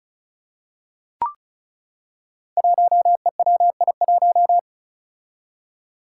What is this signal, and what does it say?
Morse code, heard as a keyed tone, about 36 wpm, 700 Hz. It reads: 1EWI1
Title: Five Challenging Letter-Number Combinations 35wpm